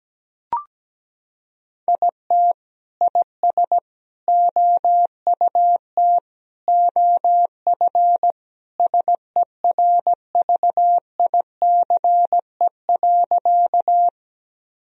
Morse code, heard as a keyed tone, about 17 wpm, 700 Hz. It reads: IT IS OUT OF SERVICE.
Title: Sentences from Top 1000 Words - Review Part 4 - 17wpm